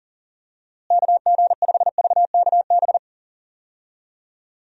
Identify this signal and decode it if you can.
Morse code, heard as a keyed tone, about 40 wpm, 700 Hz. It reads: KG5VKB
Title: USA Call Signs - 6 Characters - Repeated 3x - 40wpm